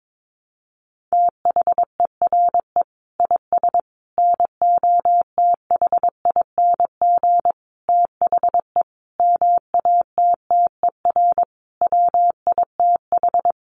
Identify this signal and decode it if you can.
Morse code, heard as a keyed tone, about 22 wpm, 700 Hz. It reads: THERE IS NOTHING THE MATTER WITH
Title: Sentences from Top 700 Words 22wpm